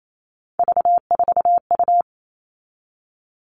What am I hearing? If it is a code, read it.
Morse code, heard as a keyed tone, about 28 wpm, 700 Hz. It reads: V4U
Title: All Cut Numbers - 3-Digit 28wpm